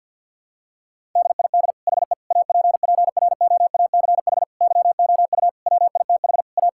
Morse code, heard as a keyed tone, about 50 wpm, 700 Hz. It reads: DID HE APPROACH YOU WITH A
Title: Sentences from Top 1000 Words - Review Part 6 - 50wpm